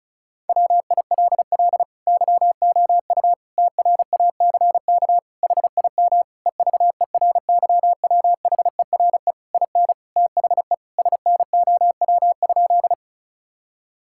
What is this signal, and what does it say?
Morse code, heard as a keyed tone, about 36 wpm, 700 Hz. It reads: WILL YOU TRACK HIM EVERYWHERE IN THE SNOW?
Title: Sentences from Top 1600 Words 35wpm